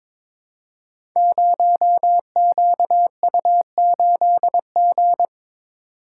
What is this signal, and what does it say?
Morse code, heard as a keyed tone, about 23 wpm, 700 Hz. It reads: 0QU8G
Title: Code Groups - 5 Letter:Numbers - 22wpm